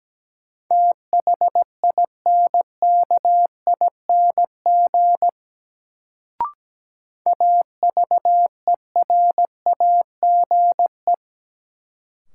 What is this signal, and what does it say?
Morse code, heard as a keyed tone, about 17 wpm, 700 Hz. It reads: THINKING AVERAGE
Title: Sentences from Top 1000 Words - Review Part 6 - New Word Drill - 17wpm